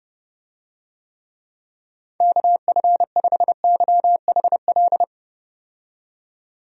Morse code, heard as a keyed tone, about 30 wpm, 700 Hz. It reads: KF5YHL
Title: USA Call Signs - 6 Characters - 30wpm